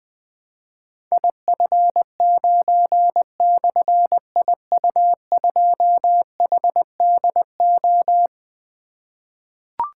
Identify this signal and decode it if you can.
Morse code, heard as a keyed tone, about 21 wpm, 700 Hz. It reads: IF9/IU2HDO
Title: Difficult Call Signs 20wpm